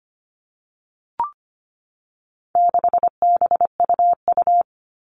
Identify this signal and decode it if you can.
Morse code, heard as a keyed tone, about 25 wpm, 700 Hz. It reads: 6BUU
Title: All Cut Numbers - 4-Digit 25wpm